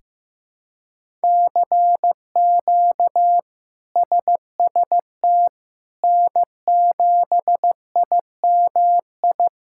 Morse code, heard as a keyed tone, about 15 wpm, 700 Hz. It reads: CQ SST N7IMI
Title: K1USN SST QSO Practice - 15wpm